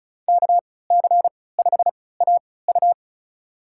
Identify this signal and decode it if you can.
Morse code, heard as a keyed tone, about 36 wpm, spaced as for 20 wpm, 690 Hz. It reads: KC5AU
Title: USA Call Signs - 5 Characters - 3x Character Spacing - 35wpm